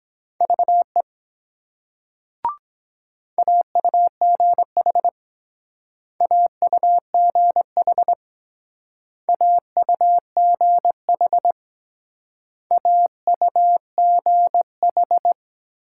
Morse code, heard as a keyed tone, about 21 wpm, 700 Hz. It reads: VE AUGH AUGH AUGH AUGH